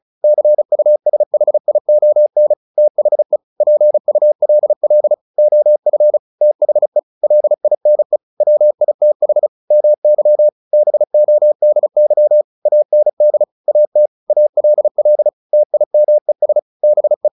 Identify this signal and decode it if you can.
Morse code, heard as a keyed tone, about 34 wpm, 600 Hz. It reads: CUSHION THE PULL OF THE LINE WITH MY BODY AND AT ALL TIMES BE
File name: the old man and the sea 2 35wpm